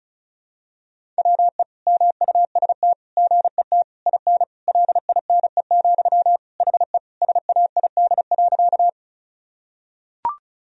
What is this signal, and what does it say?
Morse code, heard as a keyed tone, about 36 wpm, 700 Hz. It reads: WE MUST GET IN LINE, HE SAID.
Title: Sentences from Top 1000 Words - Review Part 1 - 35wpm